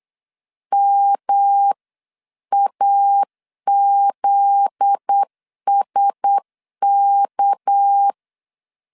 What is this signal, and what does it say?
Morse code, heard as a keyed tone, about 8 wpm, 800 Hz. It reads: MAZSK